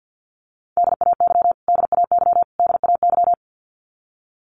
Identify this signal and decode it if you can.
Morse code, heard as a keyed tone, about 49 wpm, 700 Hz. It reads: DAY DAY DAY
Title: Top 200 Words Repeated 50wpm